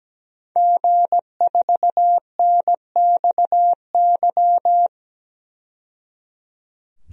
Morse code, heard as a keyed tone, about 17 wpm, 700 Hz. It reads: G4NXY
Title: DXCC-100 Call Signs - 5 Characters - 17wpm